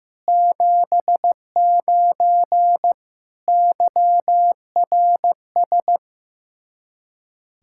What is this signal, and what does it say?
Morse code, heard as a keyed tone, about 15 wpm, 700 Hz. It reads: 79 YRS
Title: Practice QSO - Exchange 3 - More Info - 15wpm